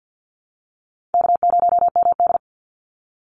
Morse code, heard as a keyed tone, about 50 wpm, 700 Hz. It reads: K0MD
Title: Call Signs Repeated 2x - 50wpm